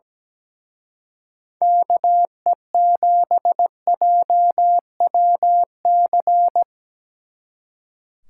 Morse code, heard as a keyed tone, about 19 wpm, 700 Hz. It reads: KE7JWC
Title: Speed Racing - USA 2x3 Call Signs 17wpm